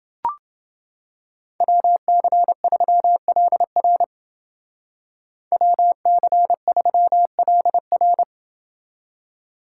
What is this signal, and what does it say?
Morse code, heard as a keyed tone, about 29 wpm, 700 Hz. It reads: WC3LR WC3LR